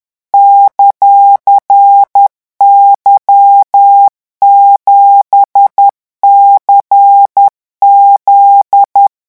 Morse code, heard as a keyed tone, about 10 wpm, 800 Hz. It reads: ;Y7CZ
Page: REVISION 18 - 10 mots minute